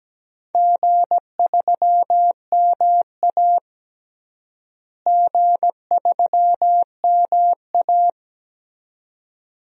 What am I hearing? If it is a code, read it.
Morse code, heard as a keyed tone, about 17 wpm, 700 Hz. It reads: G3MA G3MA
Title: DXCC-100 Call Signs - 4 Characters - Repeated 2x - 17wpm